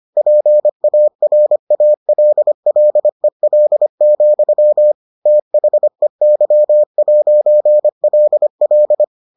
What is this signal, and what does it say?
Morse code, heard as a keyed tone, about 25 wpm, 600 Hz. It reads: PARALLEL, THEY'LL